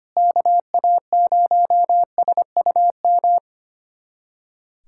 Morse code, heard as a keyed tone, about 25 wpm, 700 Hz. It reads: KA0SUM
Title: USA Callsign DE Callsign - 6 Characters - 25wpm